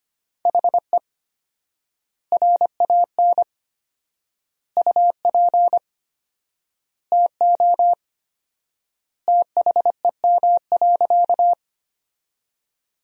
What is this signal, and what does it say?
Morse code, heard as a keyed tone, about 25 wpm, 700 Hz. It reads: HE RAN UP TO THEM.
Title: 4x Word-Spacing - Sentences from Top 500 Words - 25wpm